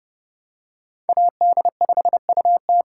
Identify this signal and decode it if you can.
Morse code, heard as a keyed tone, about 30 wpm, 700 Hz. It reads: AD5UT